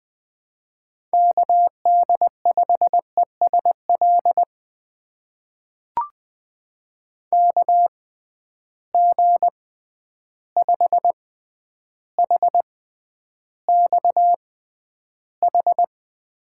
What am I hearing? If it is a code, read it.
Morse code, heard as a keyed tone, about 20 wpm, spaced as for 10 wpm, 700 Hz. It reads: KD5ESL KG5HXH
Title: USA Call Signs - 6x Character Spacing - 20wpm